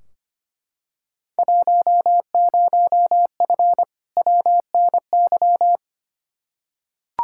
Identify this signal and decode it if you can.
Morse code, heard as a keyed tone, about 25 wpm, 700 Hz. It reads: 10F WNY
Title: Field Day Exchanges Encore - 25wpm